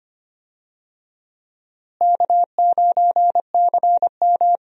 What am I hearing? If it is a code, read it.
Morse code, heard as a keyed tone, about 25 wpm, 700 Hz. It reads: K9CM
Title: USA Call Signs - 4 Characters - Repeated 2x - 25wpm